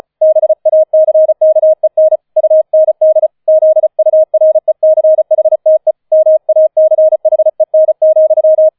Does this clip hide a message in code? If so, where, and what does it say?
Yes: Morse code from 0.2 to 8.7 s, heard as a keyed tone, about 35 wpm, 630 Hz. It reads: BACKEN UND ZURECHTE MACHEN,